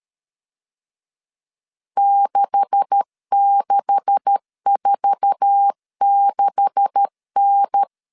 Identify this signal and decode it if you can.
Morse code, heard as a keyed tone, about 13 wpm, 800 Hz. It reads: 6646N
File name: LEZIONI / 9 286 12wpm 2305